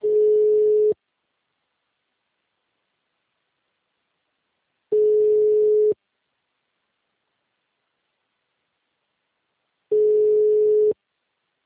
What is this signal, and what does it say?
Ringback tone (Europe, 425 Hz).
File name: out- -2041- - - .41752